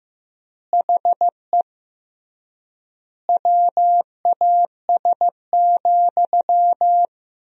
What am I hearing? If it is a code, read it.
Morse code, heard as a keyed tone, about 16 wpm, 700 Hz. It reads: HE WAS,
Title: 3x Word-Spacing - Sentences from Top 400 Words - 15wpm